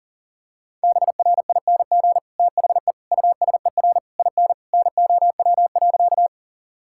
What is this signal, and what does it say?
Morse code, heard as a keyed tone, about 40 wpm, 700 Hz. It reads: BRING THE USER IN NOW.
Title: Sentences from Top 1000 Words - Review Part 5 - 40wpm